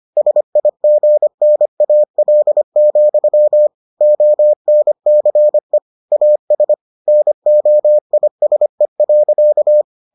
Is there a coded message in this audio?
Yes: Morse code from 0.2 to 9.8 s, heard as a keyed tone, about 25 wpm, 600 Hz. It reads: SIGNAL, ONCE AS NOISE.